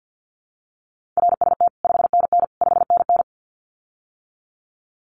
Morse code, heard as a keyed tone, about 50 wpm, 700 Hz. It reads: RST 5NN 5NN
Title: Common QSO Phrases Encore 50wpm